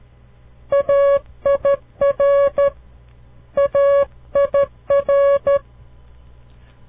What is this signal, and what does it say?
Morse code, heard as a keyed tone, about 13 wpm, 560 Hz. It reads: AIR AIR
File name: CW Learning Programs / farnsworth / unit 2